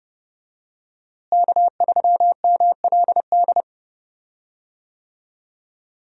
Morse code, heard as a keyed tone, about 30 wpm, 700 Hz. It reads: K3MLD
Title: USA Call Signs - 5 Characters - Repeated 2x - 30wpm